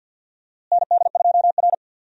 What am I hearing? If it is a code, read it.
Morse code, heard as a keyed tone, about 50 wpm, 700 Hz. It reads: ND2R
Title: USA Call Signs - 4 Characters - Repeated 3x - 50wpm